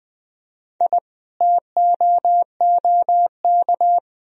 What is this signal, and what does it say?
Morse code, heard as a keyed tone, about 19 wpm, 700 Hz. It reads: I TOOK
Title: Sentences from Top 1000 Words - Review Part 2 - 20wpm